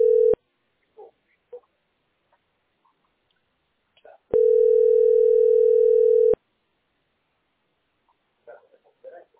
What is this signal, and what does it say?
Ringback tone (North America), 440 + 480 Hz.